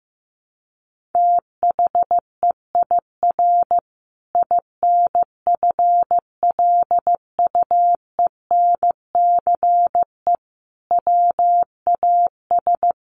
Morse code, heard as a keyed tone, about 15 wpm, 700 Hz. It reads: THEIR INFLUENCE WAS
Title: Sentences from Top 1400 Words 15wpm